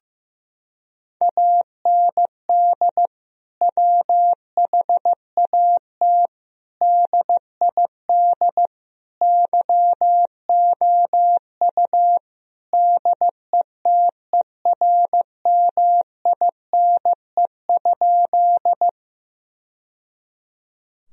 Morse code, heard as a keyed tone, about 15 wpm, 700 Hz. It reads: AND WHAT DID YOU DETERMINE?
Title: Sentences from Top 1000 Words - Review Part 4 - 15wpm